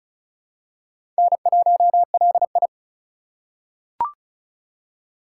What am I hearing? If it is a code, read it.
Morse code, heard as a keyed tone, about 36 wpm, 700 Hz. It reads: N1LI
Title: USA Call Signs - 4 Characters - 3x Character Spacing - 35wpm